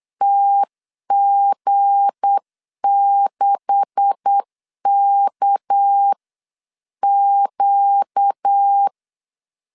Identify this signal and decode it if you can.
Morse code, heard as a keyed tone, about 8 wpm, 800 Hz. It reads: TG6KQ
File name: LEZIONI / 9 286 8wpm 3425